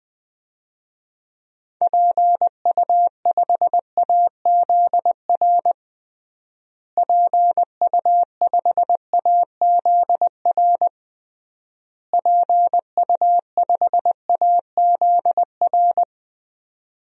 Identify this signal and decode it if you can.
Morse code, heard as a keyed tone, about 21 wpm, 700 Hz. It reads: PU5AZR PU5AZR PU5AZR
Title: DXCC-100 Call Signs - 6 Characters - Repeated 3x - 20wpm